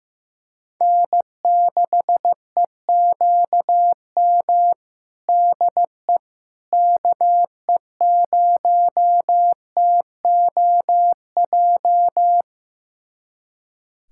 Morse code, heard as a keyed tone, about 15 wpm, 700 Hz. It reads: N6EQM DE KE0TOJ
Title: USA Callsign DE Callsign 15wpm